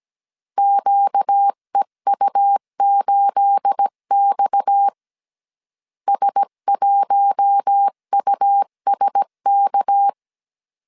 Morse code, heard as a keyed tone, about 17 wpm, 800 Hz. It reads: QEU8X S1USK
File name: 1500C S B 16wpm